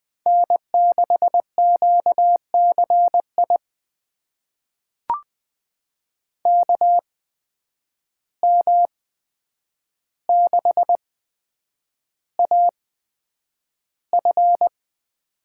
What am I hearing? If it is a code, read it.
Morse code, heard as a keyed tone, about 20 wpm, spaced as for 9 wpm, 700 Hz. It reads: N6QCI KM6AF
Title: USA Call Signs - 5 Characters - 8x Character Spacing - 20wpm